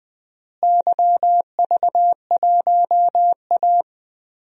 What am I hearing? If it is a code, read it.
Morse code, heard as a keyed tone, about 20 wpm, 700 Hz. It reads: YV1A
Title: DXCC-100 Call Signs - 4 Characters - Repeated 2x - 20wpm